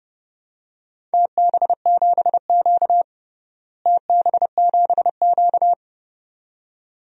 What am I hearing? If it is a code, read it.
Morse code, heard as a keyed tone, about 32 wpm, 700 Hz. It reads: TB7Q TB7Q
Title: DXCC-100 Call Signs - 4 Characters - Repeated 2x - 30wpm